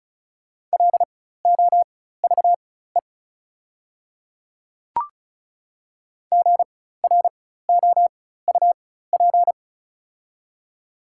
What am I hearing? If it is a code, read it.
Morse code, heard as a keyed tone, about 34 wpm, spaced as for 16 wpm, 690 Hz. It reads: LOVE GROUP
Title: 4x Character Spacing - Top 500 Words - 35wpm